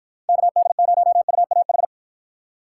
Morse code, heard as a keyed tone, about 53 wpm, 700 Hz. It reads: KD0UAH